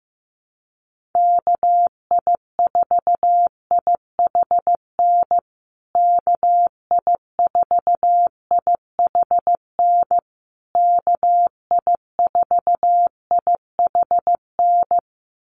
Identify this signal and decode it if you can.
Morse code, heard as a keyed tone, about 15 wpm, 700 Hz. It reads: KI4IHN KI4IHN KI4IHN
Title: USA Call Signs Repeated 3x 15wpm